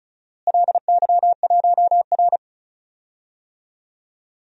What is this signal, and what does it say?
Morse code, heard as a keyed tone, about 36 wpm, 700 Hz. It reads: LY1R
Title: DXCC-100 Call Signs - 4 Characters - Repeated 2x - 35wpm